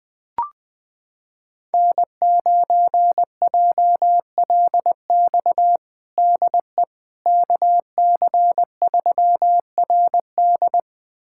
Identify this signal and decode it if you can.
Morse code, heard as a keyed tone, about 20 wpm, 700 Hz. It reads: N9JLX DE KC3RD
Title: USA Callsign DE Callsign 20wpm